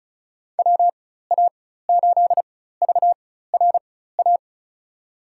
Morse code, heard as a keyed tone, about 34 wpm, spaced as for 16 wpm, 700 Hz. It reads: WA8VRA
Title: USA Call Signs - 6 Characters - 4x Character Spacing - 35wpm